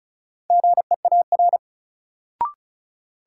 Morse code, heard as a keyed tone, about 36 wpm, 700 Hz. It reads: GEAR
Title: Sentences from Top 1400 Words - New Word Drill - 35wpm